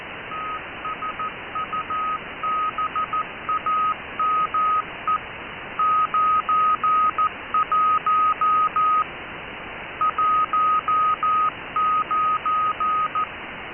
Morse code, heard as a keyed tone, about 14 wpm, 1280 Hz, with noise. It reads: TSUBAME 91 19